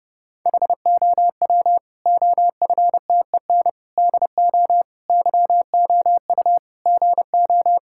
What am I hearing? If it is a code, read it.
Morse code, heard as a keyed tone, about 30 wpm, 700 Hz. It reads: HOW OFTEN DO YOU GO
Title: Sentences from Top 1000 Words - Review Part 5 - 30wpm